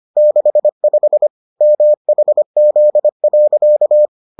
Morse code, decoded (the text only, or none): 65 MHZ.